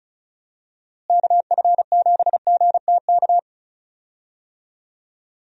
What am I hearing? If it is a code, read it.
Morse code, heard as a keyed tone, about 36 wpm, 700 Hz. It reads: KF7GTK
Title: USA Call Signs - 6 Characters - Repeated 2x - 35wpm